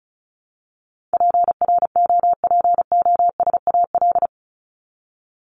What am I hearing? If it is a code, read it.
Morse code, heard as a keyed tone, about 34 wpm, 700 Hz. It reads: PROPOSAL
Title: Top 1751-2000 Words 35wpm